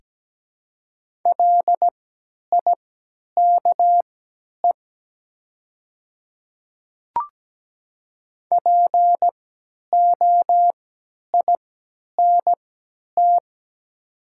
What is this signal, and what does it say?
Morse code, heard as a keyed tone, about 17 wpm, spaced as for 10 wpm, 700 Hz. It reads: LIKE POINT
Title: 3x Character Spacing - Top 200 Words - 17wpm